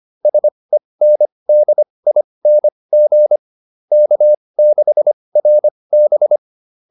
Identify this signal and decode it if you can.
Morse code, heard as a keyed tone, about 25 wpm, spaced as for 20 wpm, 600 Hz. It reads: SENDING K6RB